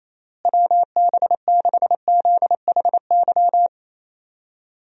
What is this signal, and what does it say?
Morse code, heard as a keyed tone, about 28 wpm, 700 Hz. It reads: WB6ZHY
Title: USA Call Signs - 6 Characters - Repeated 3x - 28wpm